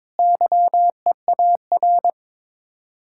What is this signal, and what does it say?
Morse code, heard as a keyed tone, about 23 wpm, 700 Hz. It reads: YEAR
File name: practice / 22wpm / Sets of 8 Words - 22wpm